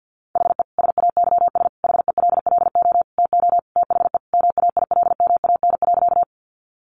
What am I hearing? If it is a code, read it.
Morse code, heard as a keyed tone, about 50 wpm, 700 Hz. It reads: HE SAYS HELLO TO THE MAILMAN.
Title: Easy Phrases 50wpm